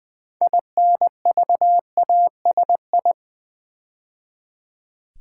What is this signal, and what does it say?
Morse code, heard as a keyed tone, about 20 wpm, 700 Hz. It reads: INVASI
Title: Build A Word - 2 Letters At A Time - 8-Letter Words - 20wpm